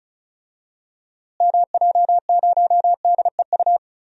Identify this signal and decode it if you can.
Morse code, heard as a keyed tone, about 34 wpm, 700 Hz. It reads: MJ0DEU
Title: DXCC-100 Call Signs - 6 Characters - Repeated 2x - 35wpm